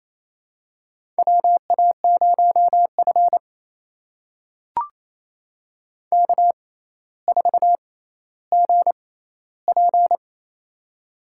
Morse code, heard as a keyed tone, about 28 wpm, spaced as for 14 wpm, 700 Hz. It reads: WA0F K4GP
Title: USA Call Signs - 4 Characters - 6x Character Spacing - 28wpm